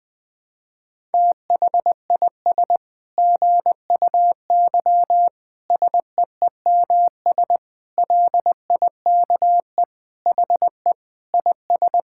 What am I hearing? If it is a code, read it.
Morse code, heard as a keyed tone, about 20 wpm, 700 Hz. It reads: THIS GUY SEEMS LIKE HE IS
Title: Sentences with 9-Word Phrases - 20wpm